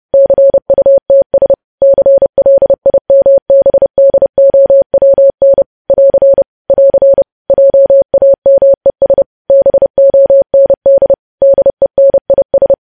Morse code, heard as a keyed tone, about 30 wpm, 550 Hz. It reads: CUTS CLIMBDOWN <AR> <AR> JAMES BOND DENIS
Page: News Headlines in Morse Code at 30 WPM